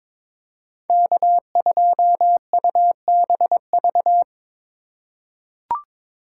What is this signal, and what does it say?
Morse code, heard as a keyed tone, about 22 wpm, 700 Hz. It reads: K2UBV
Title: USA Call Signs - 5 Characters - 3x Character Spacing - 22wpm